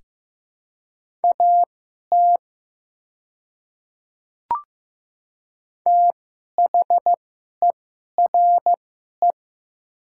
Morse code, heard as a keyed tone, about 15 wpm, spaced as for 11 wpm, 700 Hz. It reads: AT THERE
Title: 2x Character Spacing - Top 500 Words - 15wpm